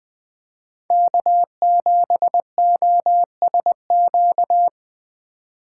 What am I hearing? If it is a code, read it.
Morse code, heard as a keyed tone, about 20 wpm, 700 Hz. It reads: K7OSQ
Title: USA Call Signs - 5 Characters - 2x Character Spacing - 20wpm